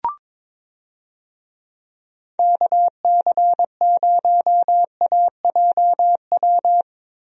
Morse code, heard as a keyed tone, about 22 wpm, 700 Hz. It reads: KC0AJW
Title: USA Call Signs - 6 Characters - Repeated 3x - 22wpm